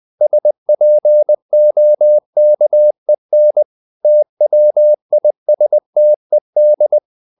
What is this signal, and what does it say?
Morse code, heard as a keyed tone, about 20 wpm, 600 Hz. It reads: SPOKEN TWISTED
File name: CW / Practice / If20